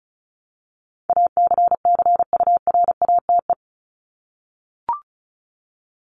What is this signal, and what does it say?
Morse code, heard as a keyed tone, about 34 wpm, 700 Hz. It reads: ACCURATE
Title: Top 1001-1250 Words 35wpm